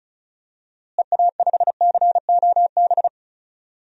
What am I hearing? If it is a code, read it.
Morse code, heard as a keyed tone, about 34 wpm, 700 Hz. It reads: EA5COB
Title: DXCC-100 Call Signs - 6 Characters - Repeated 3x - 35wpm